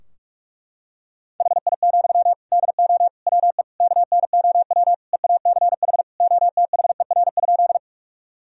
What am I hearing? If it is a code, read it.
Morse code, heard as a keyed tone, about 46 wpm, 700 Hz. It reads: HI, DO WE KNOW EACH OTHER?
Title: Sentences from Top 1000 Words - Review Part 3 - 45wpm